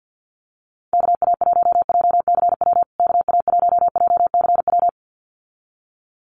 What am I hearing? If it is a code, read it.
Morse code, heard as a keyed tone, about 50 wpm, 700 Hz. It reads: KA1JCW KA1JCW
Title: USA Call Signs Repeated 2x 50wpm